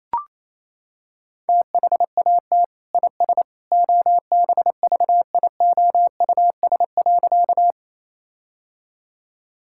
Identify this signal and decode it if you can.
Morse code, heard as a keyed tone, about 28 wpm, 700 Hz. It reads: THAT IS OBVIOUS.
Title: Sentences from Top 1600 Words 28wpm